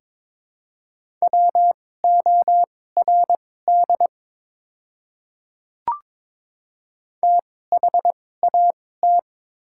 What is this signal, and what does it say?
Morse code, heard as a keyed tone, about 22 wpm, spaced as for 16 wpm, 700 Hz. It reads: WORD THAT
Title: 2x Character Spacing - Top 200 Words - 22wpm